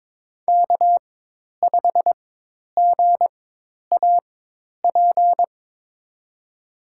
Morse code, heard as a keyed tone, about 22 wpm, spaced as for 10 wpm, 700 Hz. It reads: K5GAP
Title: USA Call Signs - 5 Characters - 4x Character Spacing - 22wpm